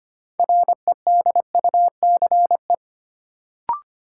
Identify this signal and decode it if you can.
Morse code, heard as a keyed tone, about 25 wpm, 700 Hz. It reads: REDUCE
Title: Sentences from Top 1100 Words - New Word Drill Alt - 25wpm